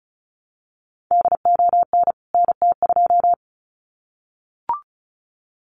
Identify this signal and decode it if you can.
Morse code, heard as a keyed tone, about 34 wpm, 700 Hz. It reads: DON NT2
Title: CWT Exchange 35wpm